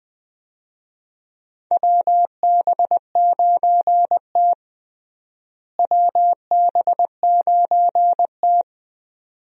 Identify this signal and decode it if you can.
Morse code, heard as a keyed tone, about 20 wpm, 700 Hz. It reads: WB9T WB9T
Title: USA Call Signs - 4 Characters - Repeated 2x - 20wpm